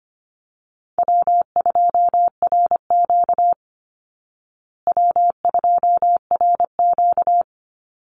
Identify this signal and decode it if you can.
Morse code, heard as a keyed tone, about 25 wpm, 700 Hz. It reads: W2RQ W2RQ
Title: Easy Call Signs Repeated 3x - 25wpm